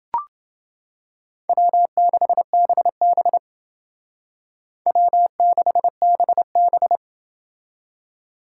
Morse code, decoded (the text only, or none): W6BB W6BB